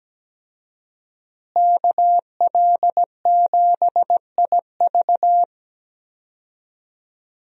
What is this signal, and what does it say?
Morse code, heard as a keyed tone, about 17 wpm, 700 Hz. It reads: KL7IV
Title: USA Call Signs - 5 Characters - Repeated 3x - 17wpm